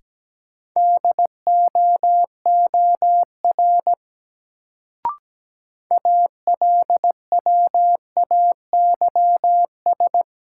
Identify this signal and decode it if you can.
Morse code, heard as a keyed tone, about 17 wpm, 700 Hz. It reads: DOOR ALWAYS
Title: Sentences from Top 300 Words - New Word Drill - 17wpm